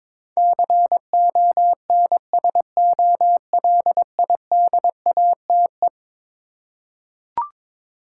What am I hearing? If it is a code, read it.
Morse code, heard as a keyed tone, about 22 wpm, 700 Hz. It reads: CONSOLIDATE
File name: practice / 22wpm / Eleven Letter Words - 22wpm